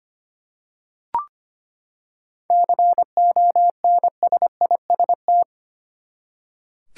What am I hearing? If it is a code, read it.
Morse code, heard as a keyed tone, about 25 wpm, 700 Hz. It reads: CONSIST
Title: Seven Letter Words - 25wpm